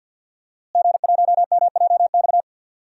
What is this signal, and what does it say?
Morse code, heard as a keyed tone, about 51 wpm, 700 Hz. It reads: G1MJX
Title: DXCC-100 Call Signs - 5 Characters - Repeated 3x - 50wpm